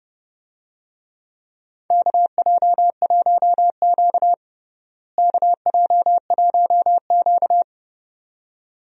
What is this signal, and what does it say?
Morse code, heard as a keyed tone, about 30 wpm, 700 Hz. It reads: KJ1Q KJ1Q
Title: USA Call Signs - 4 Characters - Repeated 2x - 30wpm